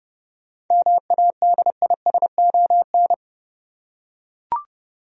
Morse code, heard as a keyed tone, about 30 wpm, 700 Hz. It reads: MADISON
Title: USA Common Names - 30wpm